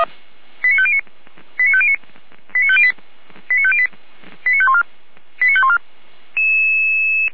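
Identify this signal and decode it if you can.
Five-tone selective call (ZVEI), heard as quick runs of five tones, tones of about 70 ms. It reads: 89509 89509 89508 89508 89514 89514
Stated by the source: ZVEI five-tone call received by SDR, fire brigade Feuerwehr Orlamünde.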